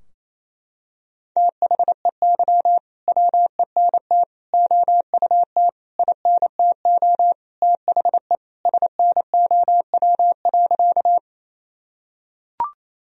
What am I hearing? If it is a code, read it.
Morse code, heard as a keyed tone, about 28 wpm, 700 Hz. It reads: THEY WENT OUT INTO THE SNOW.
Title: Sentences from Top 1000 Words - Review Part 3 - 28wpm